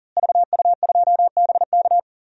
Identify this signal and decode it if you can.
Morse code, heard as a keyed tone, about 40 wpm, 700 Hz. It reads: VU2BK
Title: DXCC-100 Call Signs - 5 Characters - Repeated 3x - 40wpm